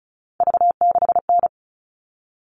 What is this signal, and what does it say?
Morse code, heard as a keyed tone, about 34 wpm, 700 Hz. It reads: V6N
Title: All Cut Numbers - 3-Digit 35wpm